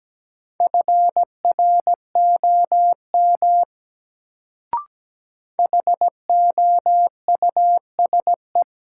Morse code, heard as a keyed tone, about 17 wpm, 700 Hz. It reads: FROM HOUSE
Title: Sets of 2 Words - New Word Drill - 17wpm